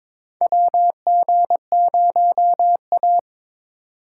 Morse code, heard as a keyed tone, about 22 wpm, 700 Hz. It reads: WG0A
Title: USA Call Signs - 4 Characters - 2x Character Spacing - 22wpm